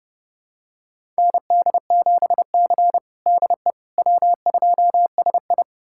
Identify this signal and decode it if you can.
Morse code, heard as a keyed tone, about 30 wpm, 700 Hz. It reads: ND7C DE W2SI
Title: USA Callsign DE Callsign - 4 Characters - 30wpm